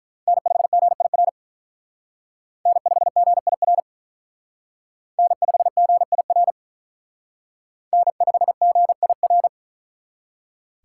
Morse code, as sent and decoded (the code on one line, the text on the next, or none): -. ..... --. .. .-. / -. ..... --. .. .-. / -. ..... --. .. .-. / -. ..... --. .. .-.
N5GIR N5GIR N5GIR N5GIR